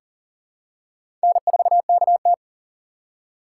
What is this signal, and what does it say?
Morse code, heard as a keyed tone, about 40 wpm, 700 Hz. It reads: N4KT
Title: USA Call Signs - 4 Characters - 6x Character Spacing - 40wpm